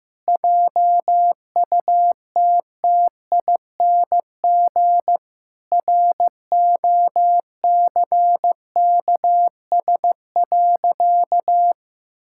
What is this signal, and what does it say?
Morse code, heard as a keyed tone, about 15 wpm, 700 Hz. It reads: JUTTING ROCKS.
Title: Pangram Challenge v2 - 15wpm